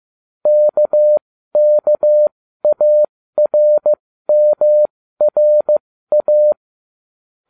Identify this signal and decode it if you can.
Morse code, heard as a keyed tone, about 15 wpm, spaced as for 13 wpm, 600 Hz. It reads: KKARMRA